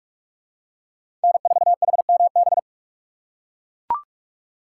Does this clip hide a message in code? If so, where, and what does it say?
Yes: Morse code from 1.2 to 2.6 s, heard as a keyed tone, about 43 wpm, 700 Hz. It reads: N4HMB